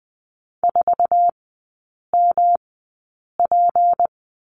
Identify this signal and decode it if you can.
Morse code, heard as a keyed tone, about 19 wpm, spaced as for 8 wpm, 700 Hz. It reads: 4MP
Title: 178 Lesson 29e - Slash - Three Characters 20wpm